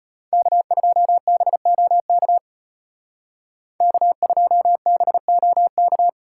Morse code, heard as a keyed tone, about 33 wpm, 700 Hz. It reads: K2BOK K2BOK